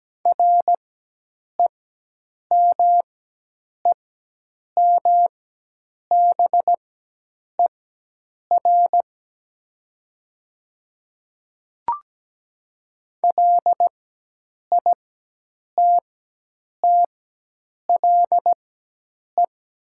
Morse code, heard as a keyed tone, about 17 wpm, spaced as for 8 wpm, 700 Hz. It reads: REMEMBER LITTLE
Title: 4x Character Spacing - Top 500 Words - 17wpm